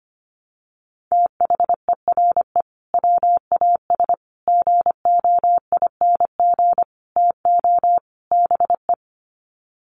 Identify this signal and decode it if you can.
Morse code, heard as a keyed tone, about 25 wpm, 700 Hz. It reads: THERE WAS GOING TO BE
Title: Sets of 5 Words Encore2 25wpm